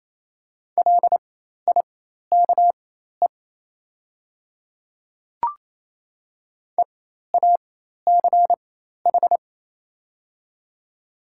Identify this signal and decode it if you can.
Morse code, heard as a keyed tone, about 28 wpm, spaced as for 13 wpm, 700 Hz. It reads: LIKE EACH
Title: 4x Character Spacing - Top 200 Words - 28wpm